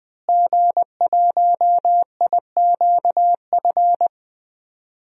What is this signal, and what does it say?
Morse code, heard as a keyed tone, about 20 wpm, 700 Hz. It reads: G1IQF
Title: DXCC-100 Call Signs - 5 Characters - Repeated 2x - 20wpm